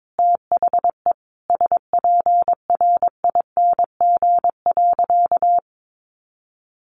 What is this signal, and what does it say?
Morse code, heard as a keyed tone, about 22 wpm, 700 Hz. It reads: THE SPRING.
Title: Sentences from Top 1200 Words 22wpm